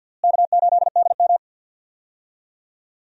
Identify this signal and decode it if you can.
Morse code, heard as a keyed tone, about 48 wpm, 700 Hz. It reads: K8DM
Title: USA Call Signs - 4 Characters - Repeated 2x - 50wpm